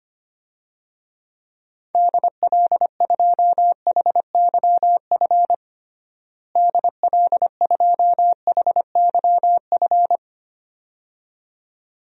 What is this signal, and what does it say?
Morse code, heard as a keyed tone, about 25 wpm, 700 Hz. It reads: DL2HYF DL2HYF
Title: DXCC-100 Call Signs - 6 Characters - Repeated 2x - 25wpm